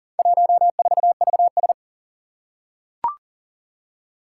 Morse code, heard as a keyed tone, about 38 wpm, 700 Hz. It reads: 14VS
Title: Four Challenging Letter-Number Combinations 40wpm